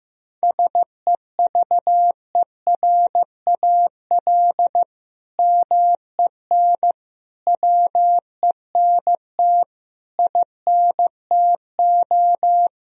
Morse code, heard as a keyed tone, about 15 wpm, 700 Hz. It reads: SEVERAL MEN WENT INTO
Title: Sentences from Top 1000 Words - Review Part 2 - 15wpm